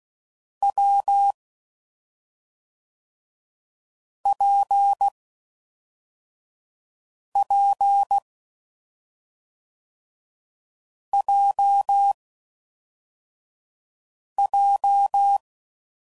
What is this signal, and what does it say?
Morse code, heard as a keyed tone, about 16 wpm, 800 Hz. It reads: W P P J J
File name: Combinations / Difficult Combos / WPJ WPJ1